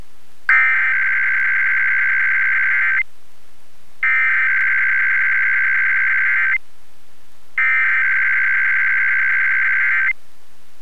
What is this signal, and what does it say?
Emergency Alert System (SAME) tones (header bursts), heard as harsh buzzing bursts. It is heard from 0.5 to 10.1 s.